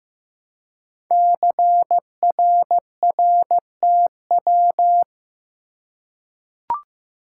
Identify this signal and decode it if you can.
Morse code, heard as a keyed tone, about 15 wpm, 700 Hz. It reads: CRRTW
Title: Code Groups - 5 Letters - 15wpm